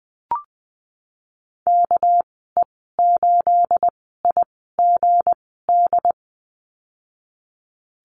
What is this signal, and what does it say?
Morse code, heard as a keyed tone, about 20 wpm, spaced as for 15 wpm, 700 Hz. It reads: KE8IGD
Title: 219 Lesson 35g - Number 0 - Call Signs 20wpm